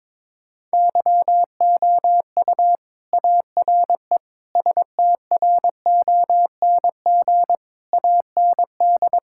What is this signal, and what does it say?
Morse code, heard as a keyed tone, about 22 wpm, 700 Hz. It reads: YOU ARE STRONG AND
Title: Sentences from Top 1000 Words - Review Part 5 - 22wpm